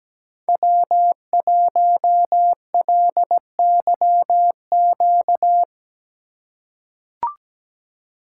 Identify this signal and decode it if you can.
Morse code, heard as a keyed tone, about 17 wpm, 700 Hz. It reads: W1LYQ
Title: USA Call Signs - 5 Characters - 2x Character Spacing - 17wpm